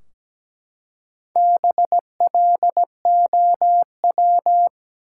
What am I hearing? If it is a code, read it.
Morse code, heard as a keyed tone, about 17 wpm, 700 Hz. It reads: BLOW
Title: Four Letter Words - 17wpm